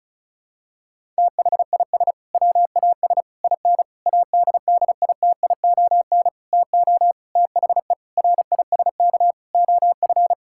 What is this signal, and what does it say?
Morse code, heard as a keyed tone, about 36 wpm, 700 Hz. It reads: THIS WAS IN ADDITION TO THE RISK OF
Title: Sentences from Top 1000 Words - Review Part 6 - 35wpm